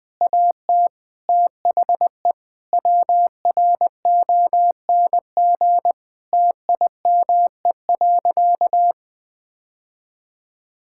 Morse code, heard as a keyed tone, about 20 wpm, 700 Hz. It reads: AT THE WRONG TIME.
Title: Sentences with 9-Word Phrases - 20wpm